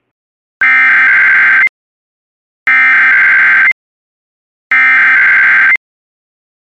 Emergency Alert System (SAME) tones (header bursts), heard as harsh buzzing bursts. It is heard from 0.6 to 5.7 s.